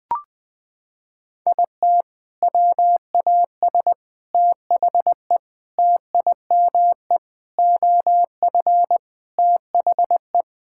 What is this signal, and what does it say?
Morse code, heard as a keyed tone, about 20 wpm, 700 Hz. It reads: IT WAS THE TIME OF THE
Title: Sentences with 4-Word Phrases - 20wpm